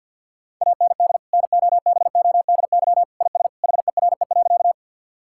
Morse code, heard as a keyed tone, about 50 wpm, 700 Hz. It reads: AND NOBODY IS HERE.
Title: Sentences from Top 1600 Words 50wpm